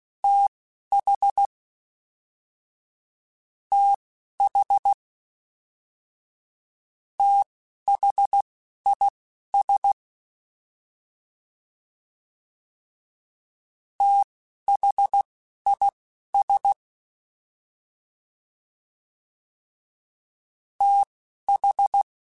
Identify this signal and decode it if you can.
Morse code, heard as a keyed tone, about 16 wpm, spaced as for 12 wpm, 800 Hz. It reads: TH TH THIS THIS TH